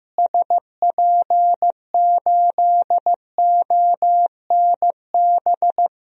Morse code, heard as a keyed tone, about 15 wpm, 700 Hz. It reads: SP8ONB
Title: DXCC-100 Call Signs - 6 Characters - Repeated 3x - 15wpm